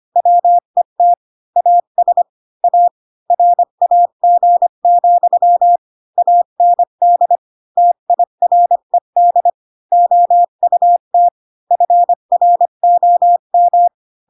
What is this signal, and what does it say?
Morse code, heard as a keyed tone, about 25 wpm, 700 Hz. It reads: WET AS A RAG, AND TIRED OUT FROM